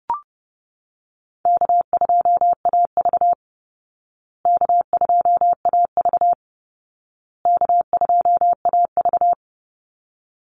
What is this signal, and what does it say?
Morse code, heard as a keyed tone, about 30 wpm, 700 Hz. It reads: K2AV K2AV K2AV
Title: Easy Call Signs Repeated 3x - 30wpm